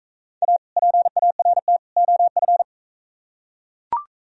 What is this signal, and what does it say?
Morse code, heard as a keyed tone, about 43 wpm, 700 Hz. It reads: A PART OF